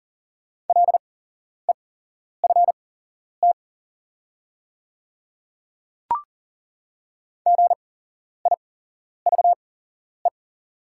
Morse code, heard as a keyed tone, about 41 wpm, spaced as for 11 wpm, 690 Hz. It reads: LEFT GIVE
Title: 8x Character Spacing - Top 300 Words - 40wpm